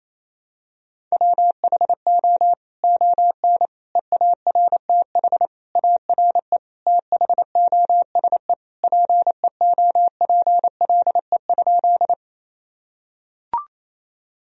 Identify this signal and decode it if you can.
Morse code, heard as a keyed tone, about 28 wpm, 700 Hz. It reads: WHO ON EARTH ARE THOSE PEOPLE?
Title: Sentences from Top 1000 Words - Review Part 1 - 28wpm